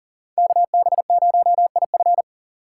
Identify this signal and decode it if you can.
Morse code, heard as a keyed tone, about 50 wpm, 700 Hz. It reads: KB0IF